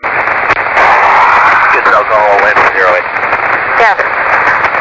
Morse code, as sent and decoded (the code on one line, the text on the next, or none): none